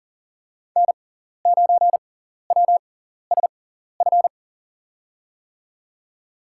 Morse code, heard as a keyed tone, about 40 wpm, spaced as for 14 wpm, 700 Hz. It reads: N9WSF
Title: USA Call Signs - 6x Character Spacing - 40wpm